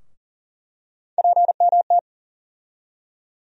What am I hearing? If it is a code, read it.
Morse code, heard as a keyed tone, about 40 wpm, 700 Hz. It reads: PMT
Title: USA Call Sign Postfixes - 3 Letters 40wpm